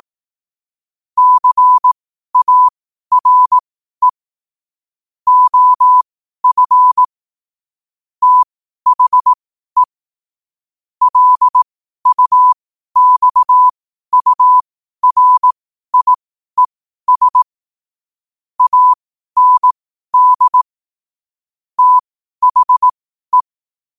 Morse code, heard as a keyed tone, about 18 wpm, spaced as for 12 wpm, 1000 Hz. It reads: CARE OF THE LUXURIES AND THE
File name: qotdfiles / 12wpm